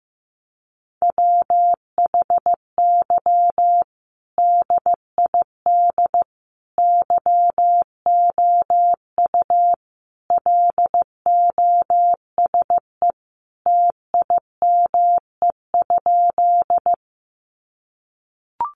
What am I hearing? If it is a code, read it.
Morse code, heard as a keyed tone, about 15 wpm, 700 Hz. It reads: WHY DID YOU LOSE TIME?
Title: Sentences from Top 1100 Words 15wpm